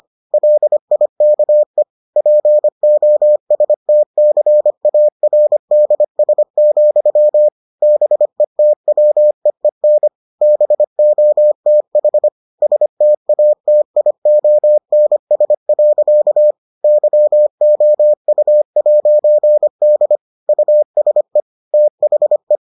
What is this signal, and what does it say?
Morse code, heard as a keyed tone, about 25 wpm, 600 Hz. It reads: LIKE POSTCARDS, BETWEEN BOTH STATIONS. YOU'D USE THE